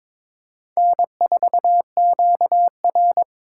Morse code, heard as a keyed tone, about 21 wpm, 700 Hz. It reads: N4QR
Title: USA Call Signs - 2x Character Spacing - 22wpm